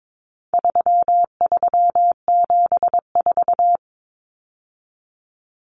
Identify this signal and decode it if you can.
Morse code, heard as a keyed tone, about 22 wpm, 700 Hz. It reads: 3374
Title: Numbers - 4-Digit 22wpm